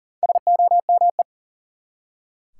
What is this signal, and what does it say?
Morse code, heard as a keyed tone, about 40 wpm, 700 Hz. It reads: SOME
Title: Top 180 QSO Elements 40wpm